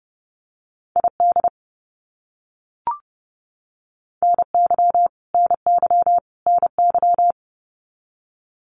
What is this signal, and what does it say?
Morse code, heard as a keyed tone, about 30 wpm, 700 Hz. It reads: ID NY NY NY
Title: US States Repeated 3x 30wpm